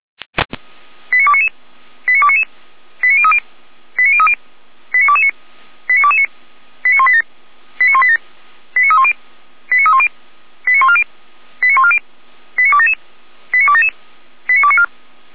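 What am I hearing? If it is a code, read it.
Five-tone selective call (ZVEI), heard as quick runs of five tones, tones of about 70 ms. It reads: 89299 89299 89030 89030 89229 89229 89187 89187 89310 89310 89140 89140 89277 89277 89294
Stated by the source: ZVEI five-tone call 89075 received by SDR, archived by feuerwehr-orlamuende.org fire brigade server.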